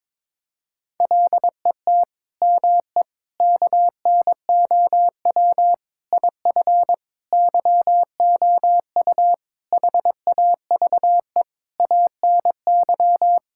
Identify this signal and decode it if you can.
Morse code, heard as a keyed tone, about 22 wpm, 700 Hz. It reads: LET ME KNOW IF YOU HAVE ANY
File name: practice / 22wpm / Sets of 8 Words - 22wpm